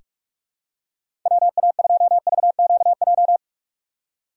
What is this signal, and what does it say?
Morse code, heard as a keyed tone, about 46 wpm, 700 Hz. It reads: WA2VQJ
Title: USA Call Signs - 6 Characters - 6x Character Spacing - 45wpm